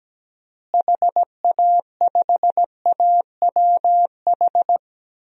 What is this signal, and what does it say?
Morse code, heard as a keyed tone, about 17 wpm, 700 Hz. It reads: HA5AWH
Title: DXCC-100 Call Signs - 6 Characters - Repeated 3x - 17wpm